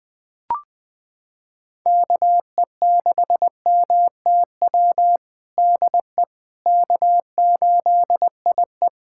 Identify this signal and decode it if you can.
Morse code, heard as a keyed tone, about 19 wpm, 700 Hz. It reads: KE6MTW DE K8IE
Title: USA Callsign DE Callsign 20wpm